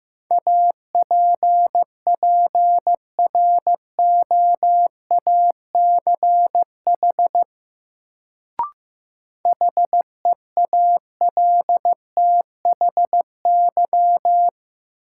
Morse code, heard as a keyed tone, about 15 wpm, 700 Hz. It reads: APPROACH HEALTHY
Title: Sentences from Top 1400 Words - New Word Drill Alt - 15wpm